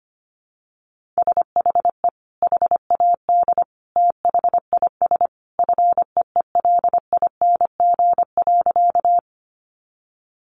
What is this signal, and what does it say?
Morse code, heard as a keyed tone, about 25 wpm, 700 Hz. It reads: SHE HAD THIS FEELING.
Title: Sentences from Top 1200 Words 25wpm